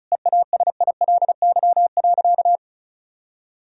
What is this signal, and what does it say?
Morse code, heard as a keyed tone, about 34 wpm, 700 Hz. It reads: EASILY.
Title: Sentences from Top 1000 Words - Review Part 1 - 35wpm